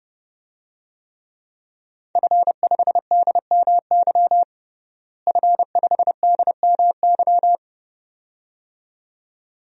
Morse code, heard as a keyed tone, about 32 wpm, 700 Hz. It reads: F5DMY F5DMY
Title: DXCC-100 Call Signs - 5 Characters - Repeated 2x - 30wpm